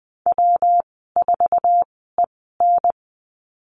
Morse code, heard as a keyed tone, about 20 wpm, spaced as for 15 wpm, 700 Hz. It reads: W4EN
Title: 154 Lesson 25g - Letter p - Call Signs 20wpm